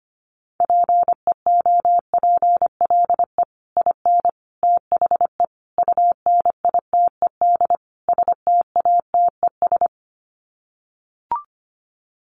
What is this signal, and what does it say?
Morse code, heard as a keyed tone, about 25 wpm, 700 Hz. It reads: PEOPLE IN THE UNITED STATES
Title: Sets of 5 Words Encore 25wpm